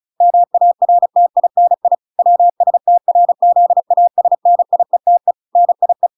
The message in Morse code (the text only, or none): MARTINI WSTRZASNIETE NIE